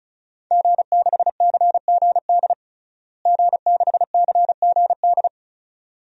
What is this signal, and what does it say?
Morse code, heard as a keyed tone, about 36 wpm, 700 Hz. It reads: G6CGD G6CGD
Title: DXCC-100 Call Signs - 5 Characters - Repeated 2x - 35wpm